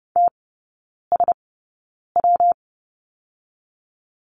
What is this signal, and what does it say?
Morse code, heard as a keyed tone, about 30 wpm, spaced as for 9 wpm, 700 Hz. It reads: TSW
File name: Lesson 19e - Letter w - Three Characters 30wpm